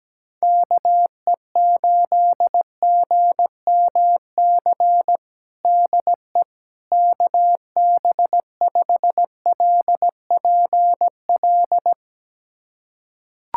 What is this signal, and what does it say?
Morse code, heard as a keyed tone, about 17 wpm, 700 Hz. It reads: KE8GMC DE KB5LPL
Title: USA Callsign DE Callsign - 6 Characters - 17wpm